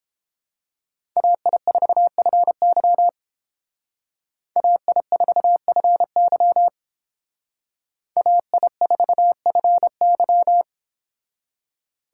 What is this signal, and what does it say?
Morse code, heard as a keyed tone, about 28 wpm, 700 Hz. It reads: AI4FY AI4FY AI4FY